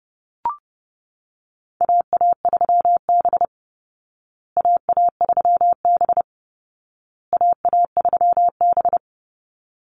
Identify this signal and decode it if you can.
Morse code, heard as a keyed tone, about 30 wpm, 700 Hz. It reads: AA3B AA3B AA3B
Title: Call Signs Repeated 2x - 30wpm